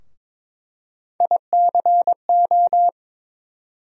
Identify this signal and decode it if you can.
Morse code, heard as a keyed tone, about 22 wpm, 700 Hz. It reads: ICO
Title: USA Call Sign Postfixes - 3 Letters 22wpm